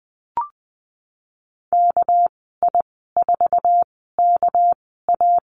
Morse code, heard as a keyed tone, about 19 wpm, spaced as for 14 wpm, 700 Hz. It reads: KI4KA
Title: 231 Lesson 37g - Letter x - Call Signs 20wpm